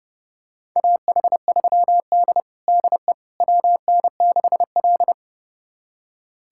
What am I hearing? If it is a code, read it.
Morse code, heard as a keyed tone, about 30 wpm, 700 Hz. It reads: AH3D DE WN6L
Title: USA Callsign DE Callsign - 4 Characters - 30wpm